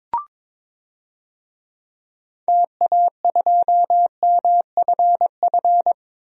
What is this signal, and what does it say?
Morse code, heard as a keyed tone, about 22 wpm, 700 Hz. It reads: TA2MFF
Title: DXCC-100 Call Signs - 6 Characters - 22wpm